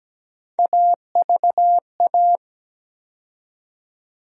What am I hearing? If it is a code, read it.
Morse code, heard as a keyed tone, about 17 wpm, 700 Hz. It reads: AVA
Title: USA Call Sign Postfixes - 3 Letters 17wpm